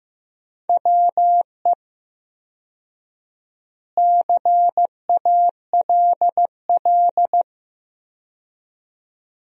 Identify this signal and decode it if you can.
Morse code, heard as a keyed tone, about 15 wpm, 700 Hz. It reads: WE CALL
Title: 4x Word-Spacing - Sentences from Top 400 Words - 15wpm